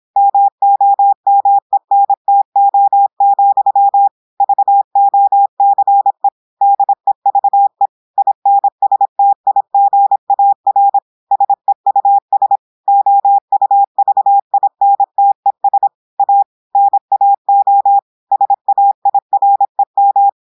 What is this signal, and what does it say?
Morse code, heard as a keyed tone, about 26 wpm, 800 Hz. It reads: MOMENTO, VOCE DEVE INSTIGAR SEUS OUVINTES A NAO SAIREM
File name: portugais texte en clair 2 vitesse 26 mots minutes